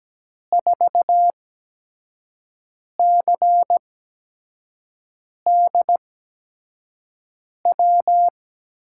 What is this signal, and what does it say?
Morse code, heard as a keyed tone, about 17 wpm, spaced as for 5 wpm, 700 Hz. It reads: 4CDW
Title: USA Call Signs - 5 Characters - 8x Character Spacing - 17wpm